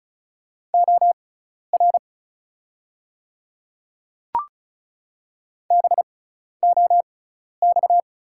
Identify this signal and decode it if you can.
Morse code, heard as a keyed tone, about 36 wpm, spaced as for 12 wpm, 700 Hz. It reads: OR BOX
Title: 6x Character Spacing - Top 500 Words - 35wpm